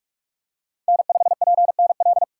Morse code, heard as a keyed tone, about 44 wpm, 700 Hz. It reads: N5PNL
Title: USA Call Signs - 5 Characters - 2x Character Spacing - 45wpm